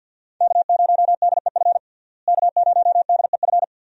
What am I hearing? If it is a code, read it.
Morse code, heard as a keyed tone, about 51 wpm, 700 Hz. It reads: K0DEF K0DEF
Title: USA Call Signs - 5 Characters - Repeated 2x - 50wpm